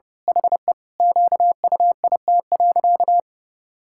Morse code, heard as a keyed tone, about 28 wpm, 700 Hz. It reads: HE QUIT.
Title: Pangram Challenge v2 - 30wpm